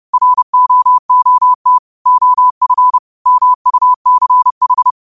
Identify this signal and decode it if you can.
Morse code, heard as a keyed tone, about 30 wpm, 1000 Hz. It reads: ROOT OF MUCH